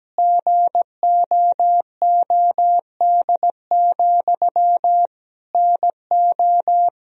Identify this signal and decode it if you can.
Morse code, heard as a keyed tone, about 17 wpm, 700 Hz. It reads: GOOD, NO
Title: Sentences from Top 1000 Words - Review Part 3 - 17wpm